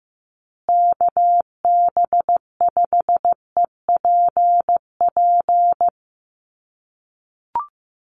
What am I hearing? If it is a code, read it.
Morse code, heard as a keyed tone, about 14 wpm, 700 Hz. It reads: KB5EPP
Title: USA Call Signs Repeated 2x 15wpm